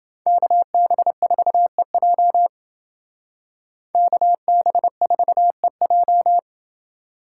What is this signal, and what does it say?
Morse code, heard as a keyed tone, about 29 wpm, 700 Hz. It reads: KB4EJ KB4EJ